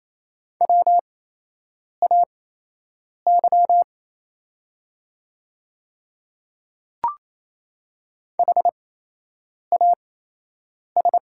Morse code, heard as a keyed tone, about 28 wpm, spaced as for 8 wpm, 700 Hz. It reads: WAY HAS
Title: 8x Character Spacing - Top 100 Words - 28wpm